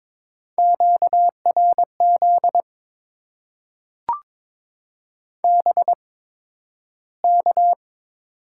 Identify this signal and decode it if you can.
Morse code, heard as a keyed tone, about 22 wpm, spaced as for 12 wpm, 700 Hz. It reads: QRZ BK
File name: Top 90 QSO Elements - 8x Character Spacing - 22wpm